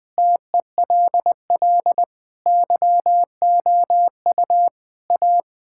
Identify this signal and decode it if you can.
Morse code, heard as a keyed tone, about 21 wpm, 700 Hz. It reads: TELL YOU A
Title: Sentences with 5-Word Phrases - 20wpm